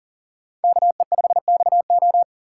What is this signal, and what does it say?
Morse code, heard as a keyed tone, about 51 wpm, 700 Hz. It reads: KE5XO